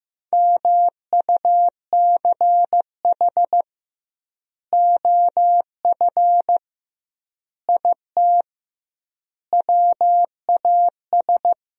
Morse code, heard as a keyed tone, about 16 wpm, 700 Hz. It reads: MUCH OF IT WAS
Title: 2x Word-Spacing - Sentences from Top 400 Words - 15wpm